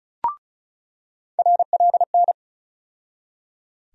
Morse code, heard as a keyed tone, about 34 wpm, 700 Hz. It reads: RLN
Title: USA Call Sign Postfixes - 3 Letters 35wpm